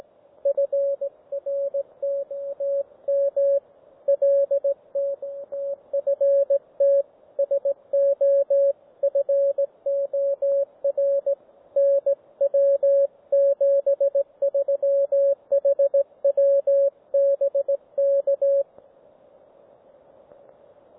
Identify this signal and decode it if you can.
Morse code, heard as a keyed tone, about 18 wpm, 560 Hz, with radio noise. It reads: FR TT M LT TFT SO FOR NW 73 HW BK
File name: GW3UEP / G3VTT OneValveTX